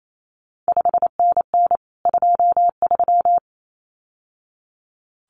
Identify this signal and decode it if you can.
Morse code, heard as a keyed tone, about 28 wpm, 700 Hz. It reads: 5NN 23
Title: CQ WW DX Contest Exchange Encore - 28wpm